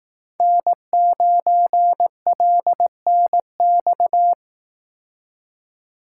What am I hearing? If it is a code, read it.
Morse code, heard as a keyed tone, about 18 wpm, 700 Hz. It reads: N9LNX
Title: Speed Racing - USA Call Signs 15wpm